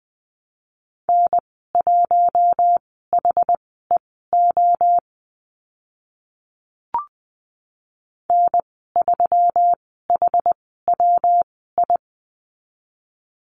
Morse code, heard as a keyed tone, about 20 wpm, spaced as for 14 wpm, 700 Hz. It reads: N1HEO N3HWI
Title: 122 Lesson 20g - Number 3 - Call Signs 20wpm